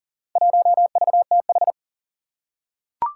Morse code, heard as a keyed tone, about 40 wpm, 700 Hz. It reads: 1VTH
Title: Four Challenging Letter-Number Combinations 40wpm